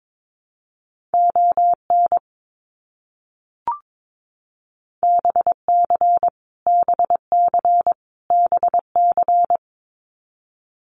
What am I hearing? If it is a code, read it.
Morse code, heard as a keyed tone, about 22 wpm, 700 Hz. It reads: ON BC BC BC
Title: Canadian Provinces and Territories - Repeated 3x - 22wpm